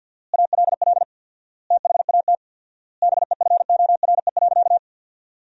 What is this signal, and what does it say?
Morse code, heard as a keyed tone, about 51 wpm, 700 Hz. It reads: ALL THAT BEFORE.
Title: 4x Word-Spacing - Sentences from Top 400 Words - 50wpm